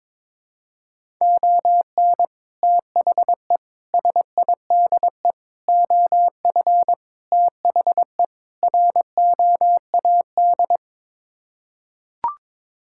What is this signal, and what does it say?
Morse code, heard as a keyed tone, about 22 wpm, 700 Hz. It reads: ON THE SIDE OF THE ROAD
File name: practice / 22wpm / Sets of 6 Words - 22wpm